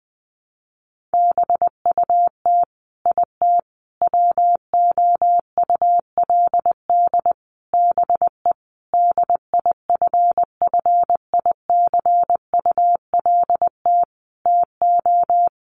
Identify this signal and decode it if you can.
Morse code, heard as a keyed tone, about 20 wpm, 700 Hz. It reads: BUT IT WOULD BE DIFFICULT TO
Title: Sentences from Top 1400 Words 20wpm